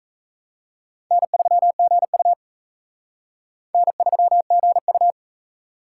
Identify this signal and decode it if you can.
Morse code, heard as a keyed tone, about 34 wpm, 700 Hz. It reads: N3GU N3GU
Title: Speed Racing - USA 1x2 Call Signs 28wpm